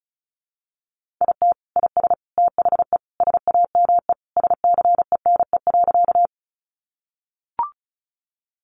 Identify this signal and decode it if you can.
Morse code, heard as a keyed tone, about 34 wpm, 700 Hz. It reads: IT IS THE SAME SCENE.
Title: Sentences from Top 1100 Words 35wpm